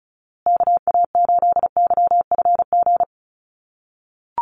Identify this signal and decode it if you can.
Morse code, heard as a keyed tone, about 34 wpm, 700 Hz. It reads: KA8YFG
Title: USA Call Signs Repeated 2x 35wpm